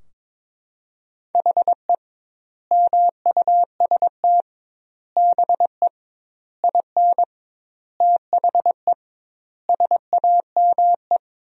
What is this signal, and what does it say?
Morse code, heard as a keyed tone, about 22 wpm, 700 Hz. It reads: HE MUST BE IN THE SAME
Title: 2x Word-Spacing - Sentences from Top 500 Words - 22wpm